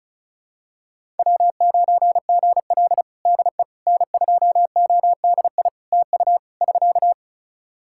Morse code, heard as a keyed tone, about 34 wpm, 700 Hz. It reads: W9GL DE N2ODI TU <SK>
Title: Practice QSO - Exchange 4 - Ending - 35wpm